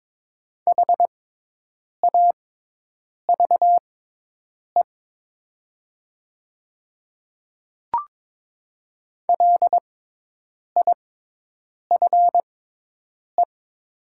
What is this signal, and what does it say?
Morse code, heard as a keyed tone, about 23 wpm, spaced as for 8 wpm, 700 Hz. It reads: HAVE LIFE
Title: 6x Character Spacing - Top 500 Words - 22wpm